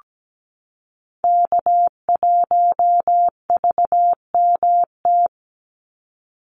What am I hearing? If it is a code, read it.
Morse code, heard as a keyed tone, about 17 wpm, 700 Hz. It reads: K1VMT
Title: Call Signs Repeated 2x - 17wpm